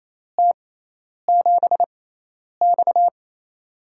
Morse code, heard as a keyed tone, about 28 wpm, spaced as for 10 wpm, 700 Hz. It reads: T7X
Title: USA Call Signs - 4 Characters - 6x Character Spacing - 28wpm